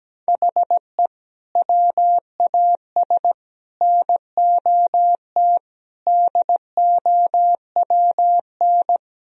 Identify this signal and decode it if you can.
Morse code, heard as a keyed tone, about 17 wpm, 700 Hz. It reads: HE WAS NOT DOWN
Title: Sentences from Top 1000 Words - Review Part 1 - 17wpm